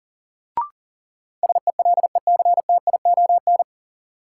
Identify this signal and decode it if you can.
Morse code, heard as a keyed tone, about 40 wpm, 700 Hz. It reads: SELECTION
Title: Sentences from Top 1400 Words - New Word Drill Alt - 40wpm